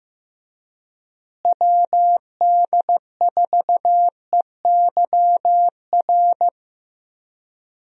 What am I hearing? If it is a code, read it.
Morse code, heard as a keyed tone, about 16 wpm, 700 Hz. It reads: WD4EYR
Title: USA Call Signs - 6 Characters - Repeated 3x - 15wpm